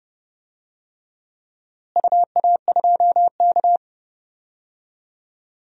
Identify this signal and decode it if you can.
Morse code, heard as a keyed tone, about 30 wpm, 700 Hz. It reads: UA2K
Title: DXCC-100 Call Signs - 4 Characters - Repeated 3x - 30wpm